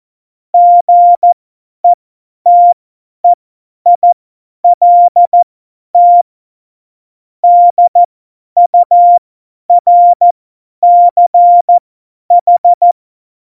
Morse code, heard as a keyed tone, about 14 wpm, spaced as for 10 wpm, 700 Hz. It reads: GETEILT DURCH
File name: Morsekurs / Witze / Witz 23. April 2019, 50 F70